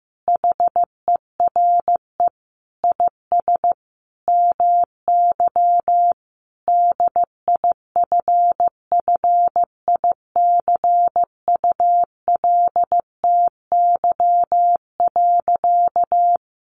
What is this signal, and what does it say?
Morse code, heard as a keyed tone, about 14 wpm, 700 Hz. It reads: HERE IS MY DIFFICULTY.
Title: Sentences from Top 1500 Words 15wpm